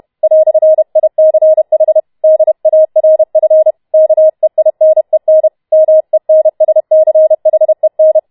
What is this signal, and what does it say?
Morse code, heard as a keyed tone, about 31 wpm, 630 Hz. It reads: "ICH DARF KEINEN MENSCHEN